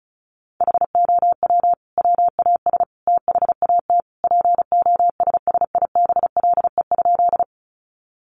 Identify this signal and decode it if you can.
Morse code, heard as a keyed tone, about 34 wpm, 700 Hz. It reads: HOW WAS THAT POSSIBLE?
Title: Sentences from Top 500 Words 35wpm